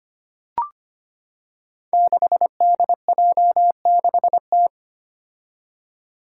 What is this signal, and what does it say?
Morse code, heard as a keyed tone, about 25 wpm, 700 Hz. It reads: 6DJ6T
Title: Five Challenging Letter-Number Combinations 25wpm